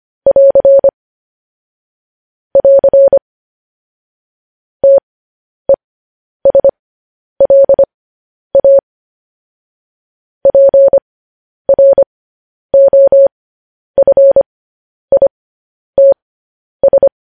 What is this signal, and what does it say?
Morse code, heard as a keyed tone, about 25 wpm, spaced as for 10 wpm, 550 Hz. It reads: <AR> <AR> TESLA PROFITS
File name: podcast / News headlines-1x-25x10